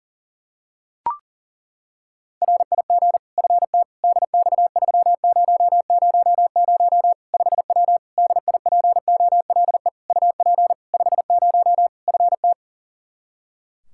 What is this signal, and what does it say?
Morse code, heard as a keyed tone, about 40 wpm, 700 Hz. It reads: RIG FT DX3000 5W DIPOLE UP 50 FT
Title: Practice QSO - Exchange 2 - Info - 40wpm